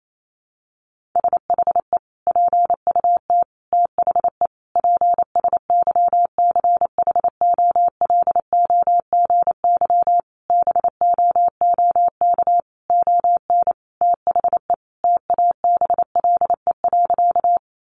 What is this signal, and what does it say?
Morse code, heard as a keyed tone, about 28 wpm, 700 Hz. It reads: SHE PUT THE PSYCHOLOGY BOOK ON THE TABLE.
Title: Sentences from Top 1400 Words 28wpm